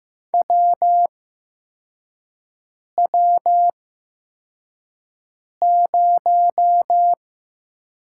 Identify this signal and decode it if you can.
Morse code, heard as a keyed tone, about 15 wpm, spaced as for 4 wpm, 700 Hz. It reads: WW0
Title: USA Call Signs - 8x Character Spacing - 15wpm